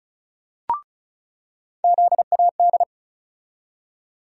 Morse code, heard as a keyed tone, about 34 wpm, 700 Hz. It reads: ZAD
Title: USA Call Sign Postfixes - 3 Letters 35wpm